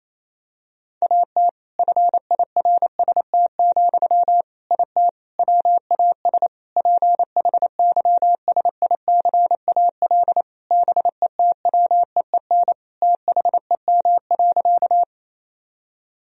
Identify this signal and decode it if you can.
Morse code, heard as a keyed tone, about 28 wpm, 700 Hz. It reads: AT FIRST, IT WAS PHYSICAL BETWEEN THEM.
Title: Sentences from Top 1000 Words - Review Part 4 - 28wpm